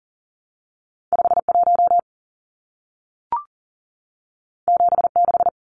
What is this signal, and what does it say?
Morse code, heard as a keyed tone, about 40 wpm, 700 Hz. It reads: 51 76
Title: Numbers - 2-Digit 40wpm